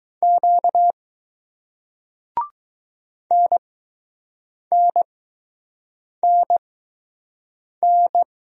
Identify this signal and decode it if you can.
Morse code, heard as a keyed tone, about 20 wpm, spaced as for 6 wpm, 700 Hz. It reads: Q NNNN